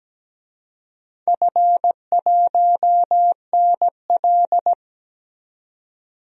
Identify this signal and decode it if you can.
Morse code, heard as a keyed tone, about 17 wpm, 700 Hz. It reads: F1NL
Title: DXCC-100 Call Signs - 4 Characters - 17wpm